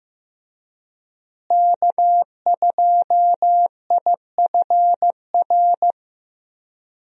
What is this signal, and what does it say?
Morse code, heard as a keyed tone, about 15 wpm, 700 Hz. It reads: K2IFR
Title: USA Call Signs - 5 Characters - Repeated 3x - 15wpm